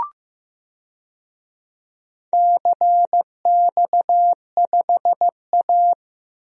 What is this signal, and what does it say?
Morse code, heard as a keyed tone, about 15 wpm, 700 Hz. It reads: CX5A
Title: DXCC-100 Call Signs - 4 Characters - 15wpm